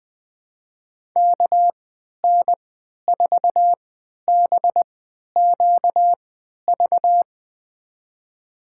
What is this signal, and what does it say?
Morse code, heard as a keyed tone, about 20 wpm, spaced as for 11 wpm, 700 Hz. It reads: KN4BQV
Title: USA Call Signs - 3x Character Spacing - 20wpm